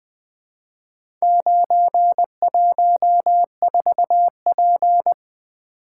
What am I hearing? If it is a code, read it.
Morse code, heard as a keyed tone, about 20 wpm, 700 Hz. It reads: 914P
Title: Four Letter-Number Combinations - 20wpm